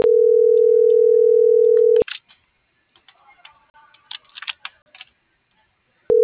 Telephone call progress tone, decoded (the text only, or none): ringback tone